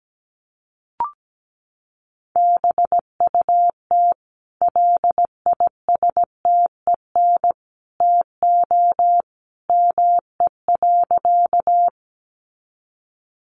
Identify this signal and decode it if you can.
Morse code, heard as a keyed tone, about 17 wpm, 700 Hz. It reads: BUT LISTEN TO ME.
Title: Sentences from Top 400 Words 17wpm